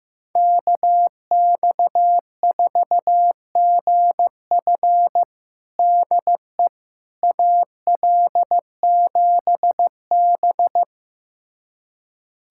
Morse code, heard as a keyed tone, about 15 wpm, 700 Hz. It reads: KX4GF DE AL7B
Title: USA Callsign DE Callsign 15wpm